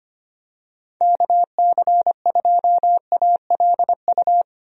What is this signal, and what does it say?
Morse code, heard as a keyed tone, about 25 wpm, 700 Hz. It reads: KC2ALU
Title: USA Call Signs - 6 Characters - Repeated 2x - 25wpm